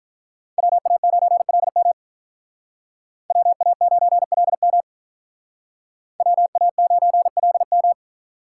Morse code, heard as a keyed tone, about 42 wpm, 700 Hz. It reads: WA9LM WA9LM WA9LM
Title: Speed Racing - USA 2x2 Call Signs 35wpm